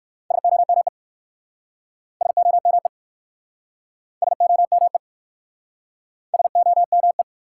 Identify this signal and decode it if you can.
Morse code, heard as a keyed tone, about 54 wpm, 700 Hz. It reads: SOME SOME SOME SOME
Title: Speed-Racing Top 180 QSO Elements 45wpm